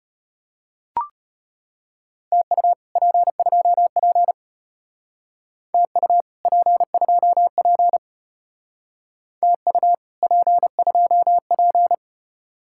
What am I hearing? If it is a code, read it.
Morse code, heard as a keyed tone, about 32 wpm, 700 Hz. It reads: TU P2P TU P2P TU P2P